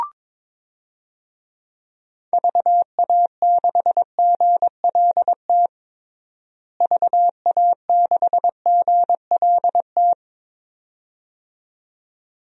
Morse code, heard as a keyed tone, about 22 wpm, 700 Hz. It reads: VA6GLT VA6GLT
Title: DXCC-100 Call Signs - 6 Characters - Repeated 2x - 22wpm